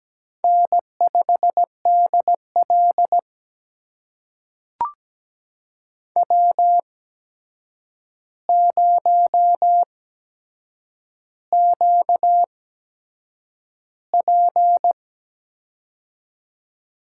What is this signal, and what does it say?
Morse code, heard as a keyed tone, about 17 wpm, spaced as for 7 wpm, 700 Hz. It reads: N5DL W0QP
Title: USA Call Signs - 4 Characters - 8x Character Spacing - 17wpm